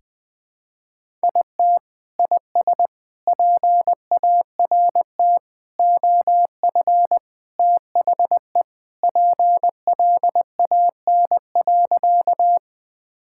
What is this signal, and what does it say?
Morse code, heard as a keyed tone, about 20 wpm, 700 Hz. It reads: IT IS PART OF THE PLAN.
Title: Sentences with 4-Word Phrases - 20wpm